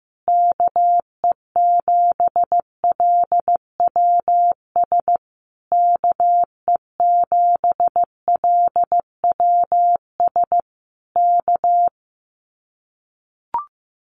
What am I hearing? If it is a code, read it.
Morse code, heard as a keyed tone, about 15 wpm, 700 Hz. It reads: KE7LWS KE7LWS K
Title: Common QSO Phrases Encore 15wpm